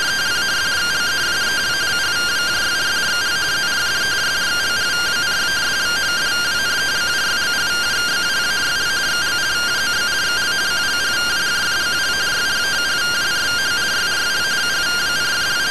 Two-tone teleprinter signal, about 170 Hz shift.